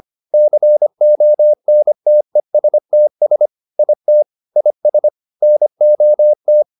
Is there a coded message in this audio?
Yes: Morse code from 0.3 to 6.6 s, heard as a keyed tone, about 25 wpm, 600 Hz. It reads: CONTESTS IT IS NOT